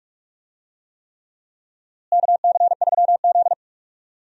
Morse code, heard as a keyed tone, about 46 wpm, 700 Hz. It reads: KC3Z
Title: USA Call Signs - 4 Characters - Repeated 3x - 45wpm